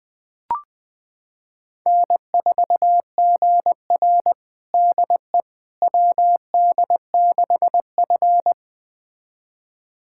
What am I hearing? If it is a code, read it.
Morse code, heard as a keyed tone, about 21 wpm, 700 Hz. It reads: N4GR DE WD6F
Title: USA Callsign DE Callsign - 4 Characters - 20wpm